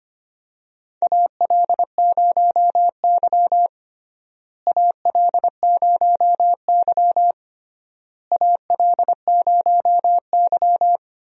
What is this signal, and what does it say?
Morse code, heard as a keyed tone, about 25 wpm, 700 Hz. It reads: AL0Y AL0Y AL0Y
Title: USA Call Signs - 4 Characters - Repeated 3x - 25wpm